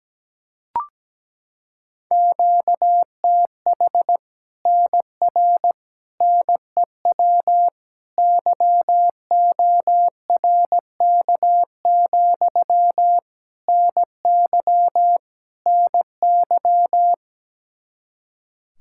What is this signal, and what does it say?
Morse code, heard as a keyed tone, about 17 wpm, 700 Hz. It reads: QTH NR NEW YORK, NY NY
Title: Practice QSO - Exchange 1 - Beginning - 17wpm